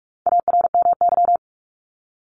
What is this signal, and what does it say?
Morse code, heard as a keyed tone, about 44 wpm, 700 Hz. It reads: ARMY
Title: Top 501-750 Words 45wpm